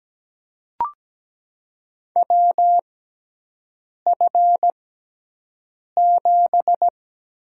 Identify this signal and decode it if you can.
Morse code, heard as a keyed tone, about 17 wpm, spaced as for 6 wpm, 700 Hz. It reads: WF7
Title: USA Call Signs - 6x Character Spacing - 17wpm